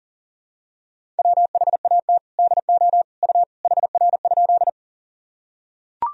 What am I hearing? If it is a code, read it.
Morse code, heard as a keyed tone, about 40 wpm, 700 Hz. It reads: WHAT DO U HR?
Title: Sentences from Top 300 Words and Abbreviations 40wpm